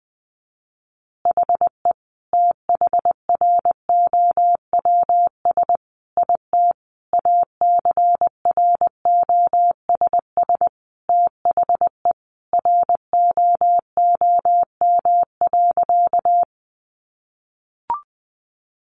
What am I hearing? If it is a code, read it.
Morse code, heard as a keyed tone, about 20 wpm, 700 Hz. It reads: HE THROWS IT ACROSS THE ROOM.
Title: Additional Easy Phrases 20wpm